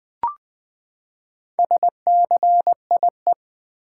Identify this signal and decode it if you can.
Morse code, heard as a keyed tone, about 20 wpm, 700 Hz. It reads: SCIE
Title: Build A Word - 4 Letters At A Time - 10-Letter Words - 20wpm